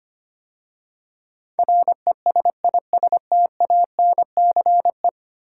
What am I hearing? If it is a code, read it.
Morse code, heard as a keyed tone, about 25 wpm, 690 Hz. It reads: RESISTANCE
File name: 10 Letter Words - First 3 Letters Given - 25wpm